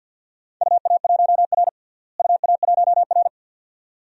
Morse code, heard as a keyed tone, about 51 wpm, 700 Hz. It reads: UA1R UA1R
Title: DXCC-100 Call Signs - 4 Characters - Repeated 2x - 50wpm